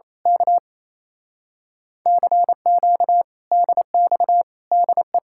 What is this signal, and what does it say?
Morse code, heard as a keyed tone, about 26 wpm, 700 Hz. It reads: K CQ DX DE
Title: Speed-Racing Common QSO Phrases Encore - 28wpm